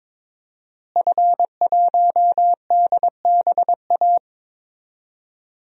Morse code, heard as a keyed tone, about 22 wpm, 700 Hz. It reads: F1DBA
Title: DXCC-100 Call Signs - 5 Characters - Repeated 3x - 22wpm